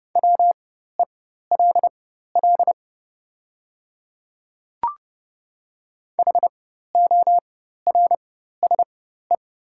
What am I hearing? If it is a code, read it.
Morse code, heard as a keyed tone, about 30 wpm, spaced as for 14 wpm, 700 Hz. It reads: WELL HORSE
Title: 4x Character Spacing - Top 300 Words - 30wpm